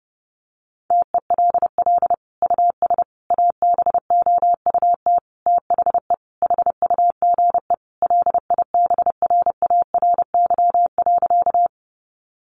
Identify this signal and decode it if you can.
Morse code, heard as a keyed tone, about 30 wpm, 700 Hz. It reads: TELL US ABOUT THE HUGE LIBRARY.
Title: Sentences from Top 1100 Words 30wpm